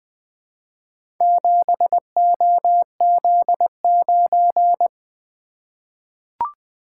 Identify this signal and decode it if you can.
Morse code, heard as a keyed tone, about 20 wpm, 700 Hz. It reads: 7OZ9
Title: Four Letter-Number Combinations - 20wpm